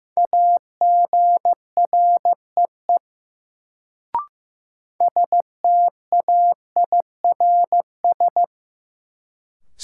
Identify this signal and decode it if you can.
Morse code, heard as a keyed tone, about 15 wpm, 700 Hz. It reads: AGREE STAIRS
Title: Sets of 5 Words Encore - New Word Drill - 15wpm